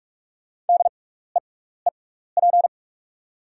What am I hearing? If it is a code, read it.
Morse code, heard as a keyed tone, about 46 wpm, spaced as for 16 wpm, 700 Hz. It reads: DEEP
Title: 6x Character Spacing - Top 500 Words - 45wpm